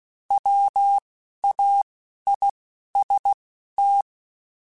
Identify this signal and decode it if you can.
Morse code, heard as a keyed tone, about 16 wpm, spaced as for 12 wpm, 800 Hz. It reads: WAIST
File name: Spaced / Chuckles Spaced-mp3s / MiddleAge sp @15